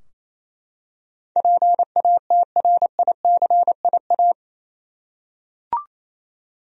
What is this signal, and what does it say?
Morse code, heard as a keyed tone, about 28 wpm, 700 Hz. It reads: PATRICIA
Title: USA Common Names - 28wpm